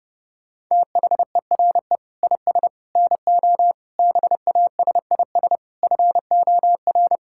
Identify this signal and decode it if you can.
Morse code, heard as a keyed tone, about 32 wpm, 700 Hz. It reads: THERE IS NO BASIS FOR
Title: Sentences from Top 1600 Words 30wpm